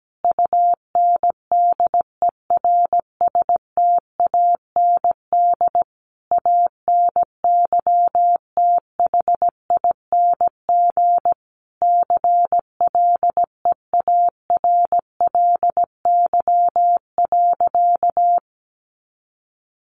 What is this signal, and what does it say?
Morse code, heard as a keyed tone, about 17 wpm, 700 Hz. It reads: UNDERSTAND ANYTHING CLEARLY.
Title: Sentences from Top 1300 Words 17wpm